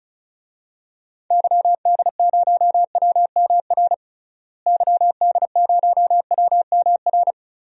Morse code, heard as a keyed tone, about 36 wpm, 700 Hz. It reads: YD0WMR YD0WMR
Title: DXCC-100 Call Signs - 6 Characters - Repeated 3x - 35wpm